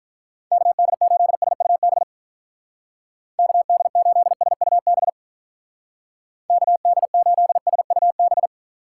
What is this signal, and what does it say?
Morse code, heard as a keyed tone, about 44 wpm, 700 Hz. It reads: KD8SUB KD8SUB KD8SUB
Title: Speed Racing - USA Call Signs 35wpm